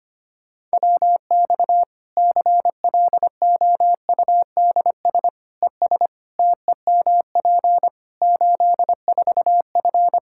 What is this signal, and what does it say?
Morse code, heard as a keyed tone, about 25 wpm, 700 Hz. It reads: WX CLOUDS ES TEMP 84F
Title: Practice QSO - Exchange 2 - Info - 25wpm